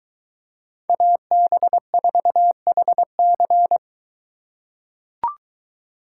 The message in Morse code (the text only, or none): AB4HC